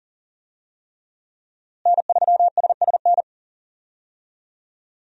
Morse code, heard as a keyed tone, about 40 wpm, 700 Hz. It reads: N3SSN
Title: USA Call Signs - 5 Characters - Repeated 3x - 40wpm